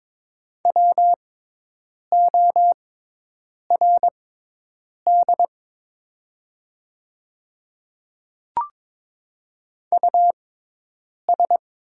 Morse code, heard as a keyed tone, about 22 wpm, spaced as for 8 wpm, 700 Hz. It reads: WORD US
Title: 6x Character Spacing - Top 300 Words - 22wpm